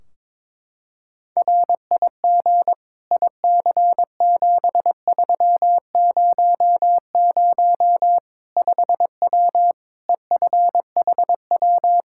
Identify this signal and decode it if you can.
Morse code, heard as a keyed tone, about 22 wpm, 700 Hz. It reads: RIG IC7300 5W EFHW
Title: Practice QSO - Exchange 2 - Info - 22wpm